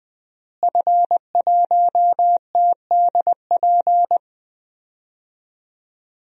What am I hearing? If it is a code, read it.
Morse code, heard as a keyed tone, about 20 wpm, 700 Hz. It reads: F1TDP
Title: DXCC-100 Call Signs - 5 Characters - Repeated 2x - 20wpm